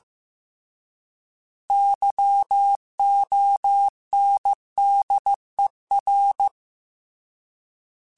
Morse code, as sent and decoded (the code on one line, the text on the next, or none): -.-- --- -. -.. .-.
YONDER